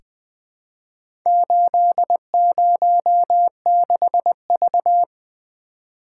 Morse code, heard as a keyed tone, about 20 wpm, 700 Hz. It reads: 806V
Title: Four Letter-Number Combinations - 20wpm